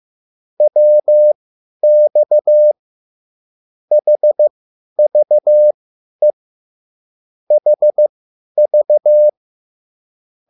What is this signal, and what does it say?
Morse code, heard as a keyed tone, about 15 wpm, spaced as for 10 wpm, 600 Hz. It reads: WX HVE HV
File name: QSL / CW lesson